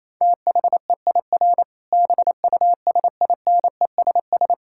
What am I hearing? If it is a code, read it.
Morse code, heard as a keyed tone, about 28 wpm, 700 Hz. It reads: THEIR BUSINESS
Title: Sentences from Top 1000 Words - Review Part 5 - 28wpm